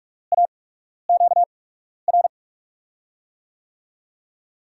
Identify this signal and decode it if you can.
Morse code, heard as a keyed tone, about 44 wpm, spaced as for 12 wpm, 700 Hz. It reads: AQR
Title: USA Call Signs - 8x Character Spacing - 45wpm